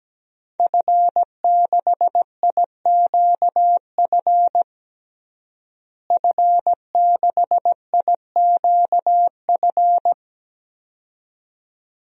Morse code, heard as a keyed tone, about 17 wpm, 700 Hz. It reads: F6IQF F6IQF
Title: DXCC-100 Call Signs - 5 Characters - Repeated 2x - 17wpm